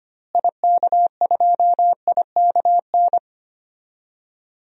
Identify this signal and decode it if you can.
Morse code, heard as a keyed tone, about 25 wpm, 700 Hz. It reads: IK2IKN
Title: DXCC-100 Call Signs - 6 Characters - Repeated 3x - 25wpm